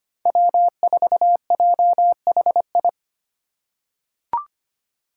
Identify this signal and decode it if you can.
Morse code, heard as a keyed tone, about 25 wpm, 700 Hz. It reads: W4JHI
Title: Five Challenging Letter-Number Combinations 25wpm